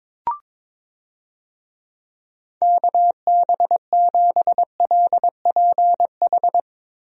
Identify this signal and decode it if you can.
Morse code, heard as a keyed tone, about 22 wpm, 700 Hz. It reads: KB7LPH
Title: USA Call Signs - 6 Characters - Repeated 3x - 22wpm